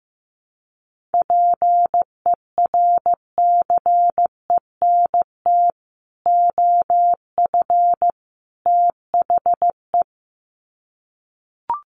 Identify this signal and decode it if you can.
Morse code, heard as a keyed tone, about 15 wpm, 700 Hz. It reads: PERCENT OF THE
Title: Sets of 3 Words 15wpm